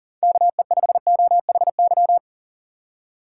Morse code, heard as a keyed tone, about 40 wpm, 700 Hz. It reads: KE5OHY
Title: USA Call Signs - 2x Character Spacing - 40wpm